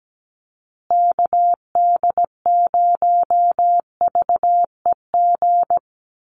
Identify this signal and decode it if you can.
Morse code, heard as a keyed tone, about 17 wpm, 700 Hz. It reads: KD0VEG
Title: USA Call Signs Repeated 2x 17wpm